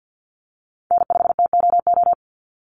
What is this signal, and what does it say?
Morse code, heard as a keyed tone, about 50 wpm, 700 Hz. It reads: N5TOO
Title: Call Signs - 50wpm